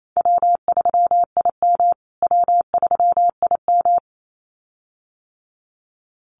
Morse code, heard as a keyed tone, about 28 wpm, 700 Hz. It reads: W3IM W3IM
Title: USA Call Signs Repeated 2x 28wpm